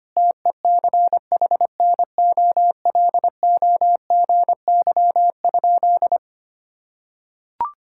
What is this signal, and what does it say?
Morse code, heard as a keyed tone, about 25 wpm, 700 Hz. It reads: TECHNOLOGY?
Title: Sentences from Top 1000 Words - Review Part 5 - 25wpm